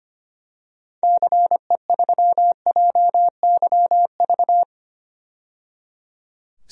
Morse code, heard as a keyed tone, about 25 wpm, 700 Hz. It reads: CE3JYV
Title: DXCC-100 Call Signs - 6 Characters - 25wpm